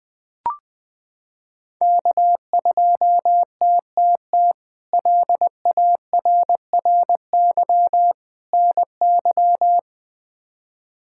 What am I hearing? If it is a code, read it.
Morse code, heard as a keyed tone, about 20 wpm, 700 Hz. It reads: K2TTT LARRY NY
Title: K1USN SST QSO Practice - 20wpm